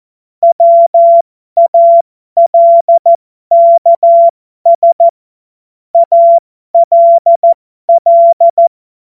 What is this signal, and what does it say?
Morse code, heard as a keyed tone, about 14 wpm, spaced as for 12 wpm, 680 Hz. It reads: WALKS ALL
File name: Morsekurs / Witze